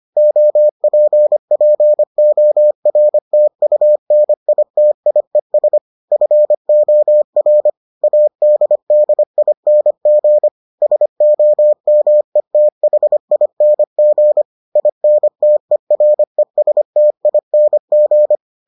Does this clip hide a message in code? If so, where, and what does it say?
Yes: Morse code from 0.2 to 18.4 s, heard as a keyed tone, about 25 wpm, 600 Hz. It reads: OPPORTUNITIES FOR ADDING SOMETHING INTERESTING